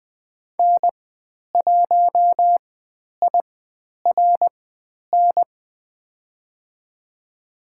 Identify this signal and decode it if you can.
Morse code, heard as a keyed tone, about 20 wpm, spaced as for 10 wpm, 700 Hz. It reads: N1IRN
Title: 109 Lesson 18g - Letter m - Call Signs 20wpm